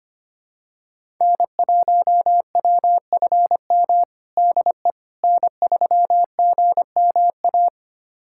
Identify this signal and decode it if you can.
Morse code, heard as a keyed tone, about 25 wpm, 700 Hz. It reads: N1WFM DE N3GMA
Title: USA Callsign DE Callsign - 5 Characters - 25wpm